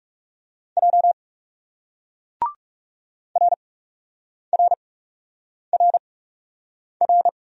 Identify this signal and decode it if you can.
Morse code, heard as a keyed tone, about 40 wpm, spaced as for 8 wpm, 700 Hz. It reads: J RRRR